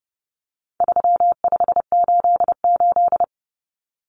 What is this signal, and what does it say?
Morse code, heard as a keyed tone, about 30 wpm, 700 Hz. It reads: 3588
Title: Common Cut Numbers - 4-Digit 30wpm